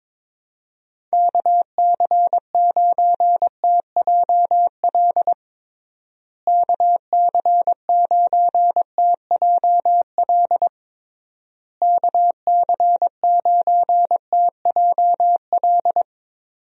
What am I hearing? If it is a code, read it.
Morse code, heard as a keyed tone, about 21 wpm, 700 Hz. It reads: KC9TJL KC9TJL KC9TJL
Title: USA Call Signs - 6 Characters - Repeated 3x - 22wpm